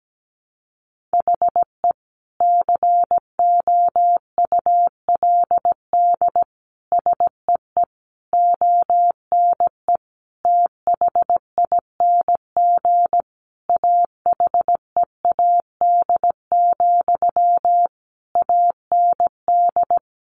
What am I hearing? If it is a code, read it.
Morse code, heard as a keyed tone, about 17 wpm, 700 Hz. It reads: HE COULD SEE ONE THING AHEAD, AND
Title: Sentences from Top 1100 Words 17wpm